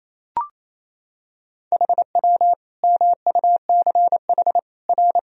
Morse code, heard as a keyed tone, about 28 wpm, 700 Hz. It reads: HW MUCH R
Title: Sentences from Top 200 Words and Abbreviations 28wpm